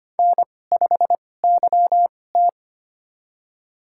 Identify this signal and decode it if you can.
Morse code, heard as a keyed tone, about 25 wpm, spaced as for 18 wpm, 700 Hz. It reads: N5YT
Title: USA Call Signs - 4 Characters - 2x Character Spacing - 25wpm